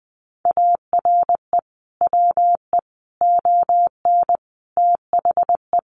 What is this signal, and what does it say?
Morse code, heard as a keyed tone, about 19 wpm, 700 Hz. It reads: ARE WE ON THE
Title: Sentences from Top 1400 Words 20wpm